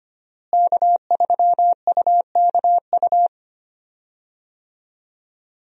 Morse code, heard as a keyed tone, about 25 wpm, 700 Hz. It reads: K3UKU
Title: USA Call Signs - 5 Characters - Repeated 3x - 25wpm